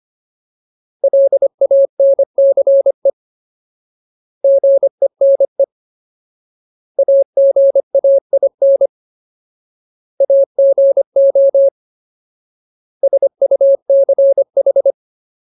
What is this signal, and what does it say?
Morse code, heard as a keyed tone, about 25 wpm, 550 Hz. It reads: LANCE GENE AGAIN AGO SUCH
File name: Ultimate Word List / 25WPM Ultimate